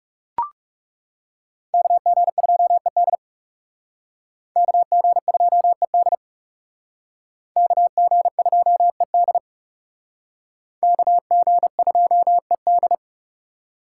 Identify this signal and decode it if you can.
Morse code, heard as a keyed tone, about 37 wpm, 700 Hz. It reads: KG2ED KG2ED KG2ED KG2ED
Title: Speed Racing - USA 2x2 Call Signs 30wpm